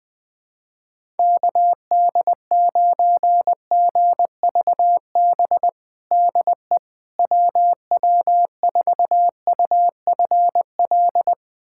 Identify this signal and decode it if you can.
Morse code, heard as a keyed tone, about 21 wpm, 700 Hz. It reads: KD9GVB DE WW4UFL
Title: USA Callsign DE Callsign - 6 Characters - 20wpm